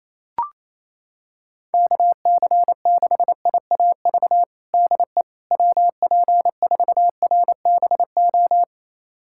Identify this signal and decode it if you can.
Morse code, heard as a keyed tone, about 28 wpm, 700 Hz. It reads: KC6IAV DE WP4RBO
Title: USA Callsign DE Callsign 28wpm